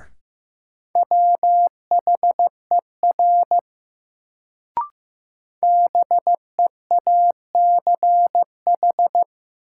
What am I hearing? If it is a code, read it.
Morse code, heard as a keyed tone, about 15 wpm, 700 Hz. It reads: WHER BEACH
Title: Sentences from Top 1500 Words - New Word Drill Alt - 15wpm